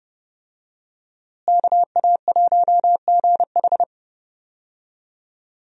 Morse code, heard as a keyed tone, about 30 wpm, 700 Hz. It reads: KA1GH
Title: USA Call Signs - 5 Characters - Repeated 3x - 30wpm